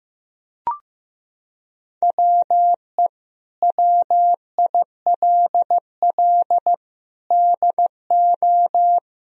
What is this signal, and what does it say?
Morse code, heard as a keyed tone, about 15 wpm, 700 Hz. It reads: WE WILL DO
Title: Sentences from Top 1000 Words - Review Part 1 - 15wpm